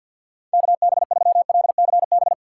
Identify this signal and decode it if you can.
Morse code, heard as a keyed tone, about 51 wpm, 700 Hz. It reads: KB3LCB
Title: USA Call Signs - 6 Characters - 6x Character Spacing - 50wpm